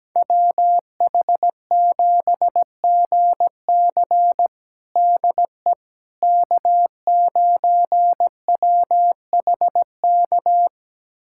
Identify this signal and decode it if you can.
Morse code, heard as a keyed tone, about 17 wpm, 700 Hz. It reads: WH7GC DE K9WHK
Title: USA Callsign DE Callsign - 5 Characters - 17wpm